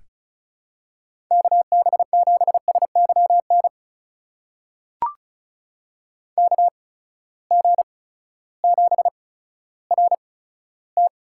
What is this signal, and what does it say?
Morse code, heard as a keyed tone, about 34 wpm, spaced as for 16 wpm, 700 Hz. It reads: KB7SYN KG7RT
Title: USA Call Signs - 6 Characters - 8x Character Spacing - 35wpm